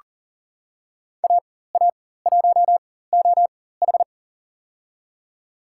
Morse code, heard as a keyed tone, about 40 wpm, spaced as for 19 wpm, 690 Hz. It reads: AA1OH
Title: USA Call Signs - 4x Character Spacing - 40wpm